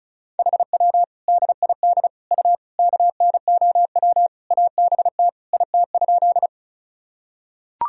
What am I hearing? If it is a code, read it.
Morse code, heard as a keyed tone, about 34 wpm, 700 Hz. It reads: HW DID U KNOW ABT IT?
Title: Sentences from Top 100 Words and Abbreviations 35wpm